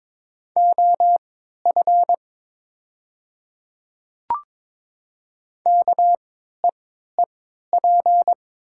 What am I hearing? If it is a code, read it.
Morse code, heard as a keyed tone, about 22 wpm, spaced as for 13 wpm, 700 Hz. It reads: OF KEEP
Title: 3x Character Spacing - Top 300 Words - 22wpm